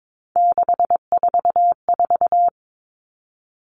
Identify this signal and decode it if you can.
Morse code, heard as a keyed tone, about 22 wpm, 700 Hz. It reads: 644
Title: Numbers - 3-Digit 22wpm